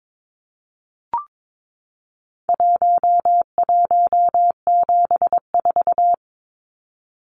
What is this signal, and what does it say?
Morse code, heard as a keyed tone, about 22 wpm, 700 Hz. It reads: 1174
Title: Numbers - Contest Style - 4-Digit 22wpm